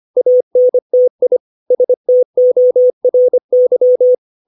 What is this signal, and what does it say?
Morse code, heard as a keyed tone, about 25 wpm, 500 Hz. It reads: ANTI STORY
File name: 632 words / 632 Words 25WPM